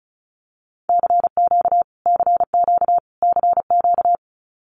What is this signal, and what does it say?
Morse code, heard as a keyed tone, about 34 wpm, 700 Hz. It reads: CQ CQ CQ
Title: Abbreviations Repeated 3x - 35wpm